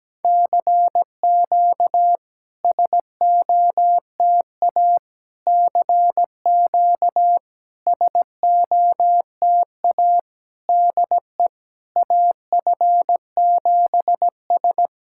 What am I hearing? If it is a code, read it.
Morse code, heard as a keyed tone, about 17 wpm, 700 Hz. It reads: CQ SOTA CQ SOTA DE AF7S
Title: SOTA QSO Practice 17wpm